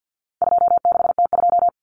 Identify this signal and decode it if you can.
Morse code, heard as a keyed tone, about 50 wpm, 700 Hz. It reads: 26T2
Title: Common Cut Numbers - 4-Digit 50wpm